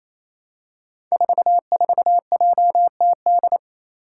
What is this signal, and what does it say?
Morse code, heard as a keyed tone, about 27 wpm, 700 Hz. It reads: 44JTD
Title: Five Challenging Letter-Number Combinations 28wpm